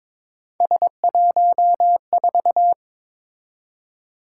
Morse code, heard as a keyed tone, about 22 wpm, 700 Hz. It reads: S14
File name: practice / 22wpm / Three Letter-Number Combinations - 22wpm